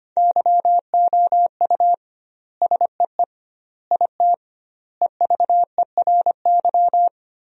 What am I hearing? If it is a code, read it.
Morse code, heard as a keyed tone, about 25 wpm, 700 Hz. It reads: YOU SEE IT EVERY
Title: 2x Word-Spacing - Sentences from Top 200 Words - 25wpm